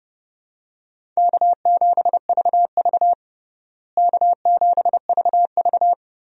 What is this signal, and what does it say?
Morse code, heard as a keyed tone, about 32 wpm, 700 Hz. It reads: K7VV K7VV
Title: USA Call Signs - 4 Characters - Repeated 3x - 30wpm